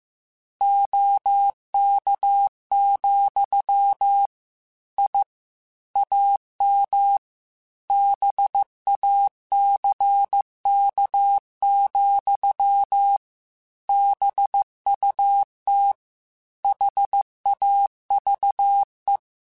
Morse code, heard as a keyed tone, about 15 wpm, 800 Hz. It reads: OK, I AM BACK, BUT HAVE